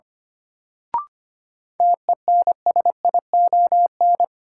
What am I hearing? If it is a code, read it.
Morse code, heard as a keyed tone, about 25 wpm, 700 Hz. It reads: TENSION
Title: Sentences from Top 1600 Words - New Word Drill - 25wpm